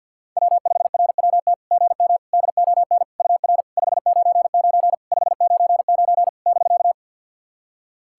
Morse code, heard as a keyed tone, about 52 wpm, 690 Hz. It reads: W5RWT GM DON UR 599 599 <BK>
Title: SOTA QSO Practice 50wpm